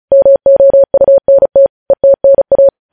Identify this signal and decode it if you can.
Morse code, heard as a keyed tone, about 36 wpm, 550 Hz. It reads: MOUNT ETNA
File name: podcast / News headlines-1x-35x35